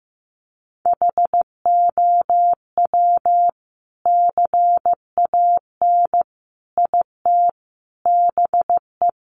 Morse code, heard as a keyed tone, about 15 wpm, 700 Hz. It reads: HOW CAN IT BE
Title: Sentences from Top 1400 Words 15wpm